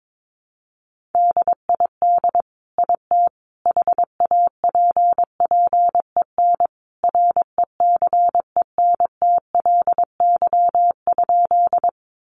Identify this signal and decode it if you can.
Morse code, heard as a keyed tone, about 22 wpm, 700 Hz. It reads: DID IT HAPPEN RECENTLY?
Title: Sentences from Top 1200 Words 22wpm